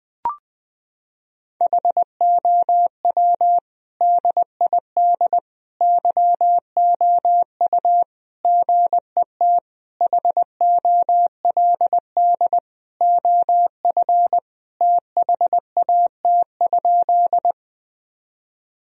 Morse code, heard as a keyed tone, about 20 wpm, 700 Hz. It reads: HOW DID YOU GET HOLD OF THAT?
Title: Sentences from Top 1000 Words - Review Part 2 - 20wpm